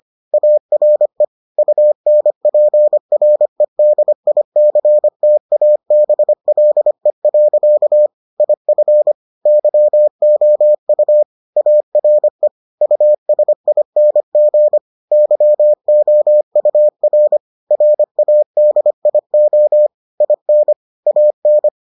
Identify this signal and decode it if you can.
Morse code, heard as a keyed tone, about 25 wpm, 600 Hz. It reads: ARE UNPREDICTABLE. IF YOU ARE USING YOUR RADIO IN AN